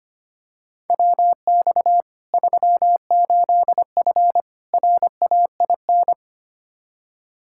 Morse code, heard as a keyed tone, about 25 wpm, 700 Hz. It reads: WX 38F RAIN
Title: Practice QSO - Exchange 2 - Info - 25wpm